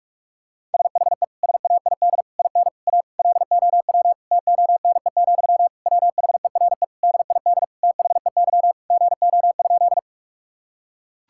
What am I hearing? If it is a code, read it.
Morse code, heard as a keyed tone, about 46 wpm, 700 Hz. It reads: SHE SAID IN A LOW TONE, WHERE DID THEY GO?
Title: Sentences from Top 1600 Words 45wpm